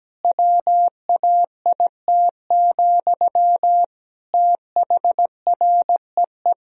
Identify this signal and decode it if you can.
Morse code, heard as a keyed tone, about 17 wpm, 700 Hz. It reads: WAIT, THREE
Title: Sentences from Top 1000 Words - Review Part 2 - 17wpm